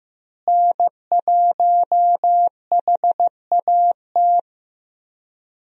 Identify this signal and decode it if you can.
Morse code, heard as a keyed tone, about 15 wpm, 700 Hz. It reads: N1HAT
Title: USA Call Signs - 5 Characters - Repeated 2x - 15wpm